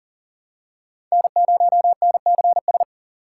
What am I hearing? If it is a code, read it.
Morse code, heard as a keyed tone, about 37 wpm, 700 Hz. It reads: N0NCS
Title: USA Call Signs - 5 Characters - 3x Character Spacing - 40wpm